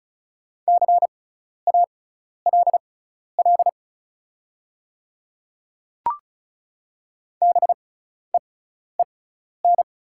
Morse code, heard as a keyed tone, about 36 wpm, spaced as for 12 wpm, 690 Hz. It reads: CALL BEEN
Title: 6x Character Spacing - Top 100 Words - 35wpm